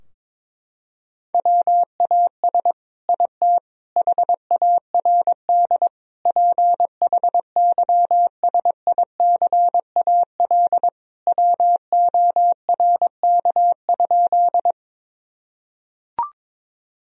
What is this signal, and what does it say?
Morse code, heard as a keyed tone, about 22 wpm, 700 Hz. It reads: WAS IT HARD PHYSICAL WORK?
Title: Sentences from Top 1000 Words - Review Part 4 - 22wpm